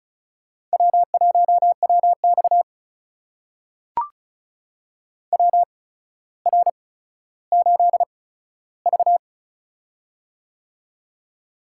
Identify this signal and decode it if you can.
Morse code, heard as a keyed tone, about 36 wpm, spaced as for 15 wpm, 700 Hz. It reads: W1WX WR8V
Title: USA Call Signs - 4 Characters - 8x Character Spacing - 35wpm